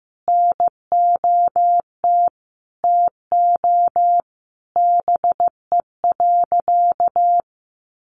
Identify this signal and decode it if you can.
Morse code, heard as a keyed tone, about 15 wpm, 700 Hz. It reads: NOT TO BE.
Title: Sentences from Top 600 Words 15wpm